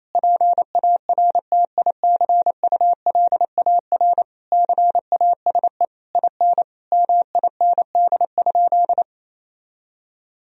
Morse code, heard as a keyed tone, about 28 wpm, 700 Hz. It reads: PARTICULAR CASE IN MIND?
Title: Sentences from Top 1000 Words - Review Part 4 - 28wpm